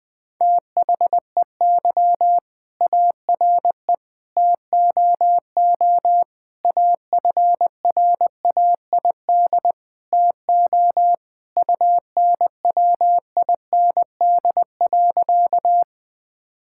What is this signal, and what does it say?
Morse code, heard as a keyed tone, about 21 wpm, 700 Hz. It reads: THEY ARE TOO AFRAID TO UNWIND.
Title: All-Age Appropriate Jokes - 20wpm